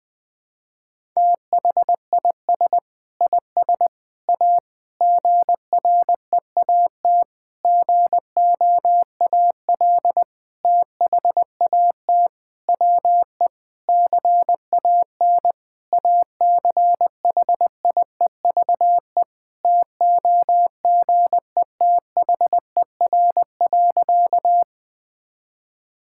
Morse code, heard as a keyed tone, about 20 wpm, 700 Hz. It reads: THIS IS A GREAT GOAL THAT WE CAN ACHIEVE TOGETHER.
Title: Sentences from Top 1000 Words - Review Part 6 - 20wpm